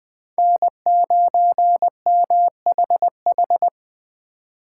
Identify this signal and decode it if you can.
Morse code, heard as a keyed tone, about 25 wpm, 700 Hz. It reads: N9MHH
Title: Speed Racing - USA 1x3 Call Signs 20wpm